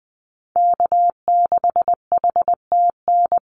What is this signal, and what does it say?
Morse code, heard as a keyed tone, about 20 wpm, 700 Hz. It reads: K6HTN
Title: Call Signs - 20wpm